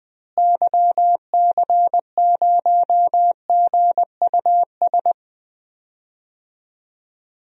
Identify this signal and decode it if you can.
Morse code, heard as a keyed tone, about 20 wpm, 700 Hz. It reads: YC0GUS
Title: DXCC-100 Call Signs - 6 Characters - Repeated 3x - 20wpm